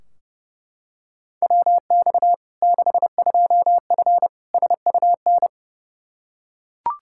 Morse code, heard as a keyed tone, about 30 wpm, 700 Hz. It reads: WX 62F SUN
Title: Practice QSO - Exchange 2 - Info - 30wpm